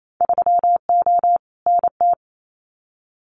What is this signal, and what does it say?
Morse code, heard as a keyed tone, about 28 wpm, 700 Hz. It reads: 3O NT
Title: Winter Field Day Exchanges Encore 28wpm